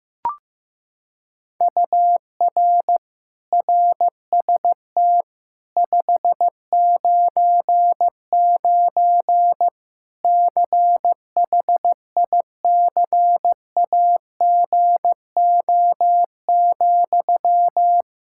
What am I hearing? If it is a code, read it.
Morse code, heard as a keyed tone, about 15 wpm, 700 Hz. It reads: UR RST 599 CHICAGO,
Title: Practice QSO - Exchange 1 - Beginning - 15wpm